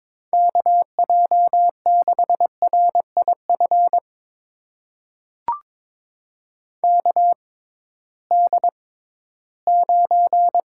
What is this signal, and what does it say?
Morse code, heard as a keyed tone, about 21 wpm, spaced as for 14 wpm, 700 Hz. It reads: KJ6RIF KD9
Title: USA Call Signs - 6 Characters - 6x Character Spacing - 22wpm